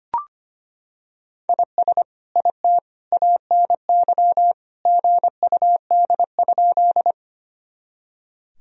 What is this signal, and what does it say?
Morse code, heard as a keyed tone, about 25 wpm, 700 Hz. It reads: IS IT ANY GUD?
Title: Sentences from Top 200 Words and Abbreviations 25wpm